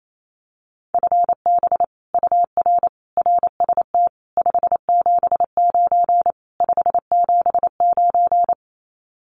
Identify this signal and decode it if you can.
Morse code, heard as a keyed tone, about 28 wpm, 700 Hz. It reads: FB UR RST 579 579
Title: Common QSO Phrases Encore 28wpm